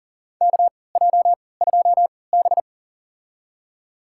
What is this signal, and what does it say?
Morse code, heard as a keyed tone, about 40 wpm, spaced as for 23 wpm, 700 Hz. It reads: KJ2B
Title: USA Call Signs - 4 Characters - 3x Character Spacing - 40wpm